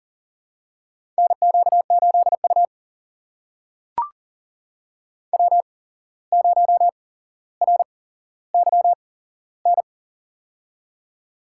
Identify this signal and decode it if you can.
Morse code, heard as a keyed tone, about 40 wpm, spaced as for 16 wpm, 700 Hz. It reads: NQ8U W0RYN
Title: USA Call Signs - 8x Character Spacing - 40wpm